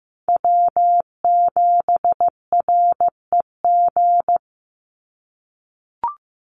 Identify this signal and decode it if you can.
Morse code, heard as a keyed tone, about 15 wpm, 700 Hz. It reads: W7REG
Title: USA Call Signs 15wpm